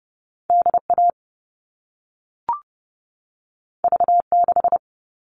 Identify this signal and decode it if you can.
Morse code, heard as a keyed tone, about 30 wpm, 700 Hz. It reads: DA V6
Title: All Cut Numbers - 2-Digit 30wpm